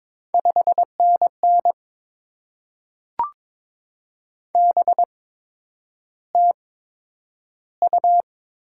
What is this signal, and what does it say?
Morse code, heard as a keyed tone, about 22 wpm, spaced as for 9 wpm, 700 Hz. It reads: 5NN BTU
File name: Top 20 QSO Elements - 8x Character Spacing - 22wpm